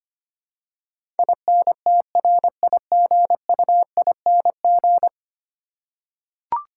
Morse code, heard as a keyed tone, about 25 wpm, 700 Hz. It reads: INTRIGUING
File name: Ten Letter Words - 25wpm